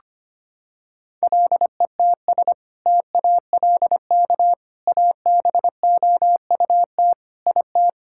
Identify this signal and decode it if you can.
Morse code, heard as a keyed tone, about 25 wpm, 700 Hz. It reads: LETS TALK ABOUT IT
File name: practice / 25wpm / Sentences with 2-Word Phrases - 25wpm